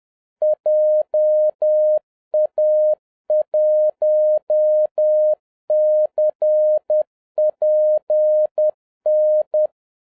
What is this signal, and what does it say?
Morse code, heard as a keyed tone, about 10 wpm, 610 Hz. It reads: JA1CPN